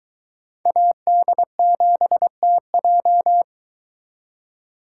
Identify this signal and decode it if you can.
Morse code, heard as a keyed tone, about 22 wpm, 700 Hz. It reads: AD7TJ